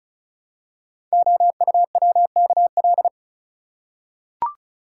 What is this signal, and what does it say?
Morse code, heard as a keyed tone, about 36 wpm, 700 Hz. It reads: OUWKL
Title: Code Groups - 5 Letters - 35wpm